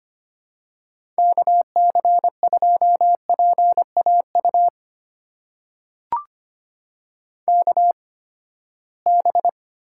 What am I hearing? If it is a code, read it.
Morse code, heard as a keyed tone, about 25 wpm, spaced as for 17 wpm, 700 Hz. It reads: KC2PAU KB
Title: USA Call Signs - 6 Characters - 8x Character Spacing - 25wpm